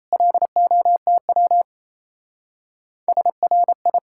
Morse code, heard as a keyed tone, about 34 wpm, 700 Hz. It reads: LOTW SRI
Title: Speed-Racing Common QSO Phrases - 28wpm